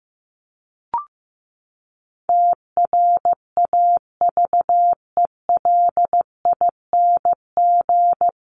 Morse code, heard as a keyed tone, about 15 wpm, 700 Hz. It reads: TRAVELING
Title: A Sound Of Thunder 15wpm